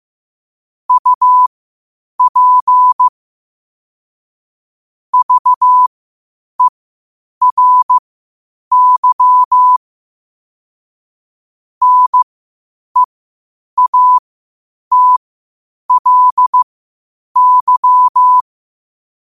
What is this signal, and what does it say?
Morse code, heard as a keyed tone, about 15 wpm, spaced as for 8 wpm, 1000 Hz. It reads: UP VERY NEATLY
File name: qotdfiles / 8wpm